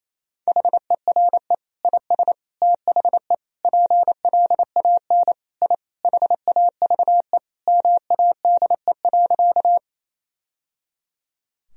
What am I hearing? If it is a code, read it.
Morse code, heard as a keyed tone, about 27 wpm, 700 Hz. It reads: HERE IS THE PLAN I HAVE MADE.
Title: Sentences from Top 1000 Words - Review Part 2 - 28wpm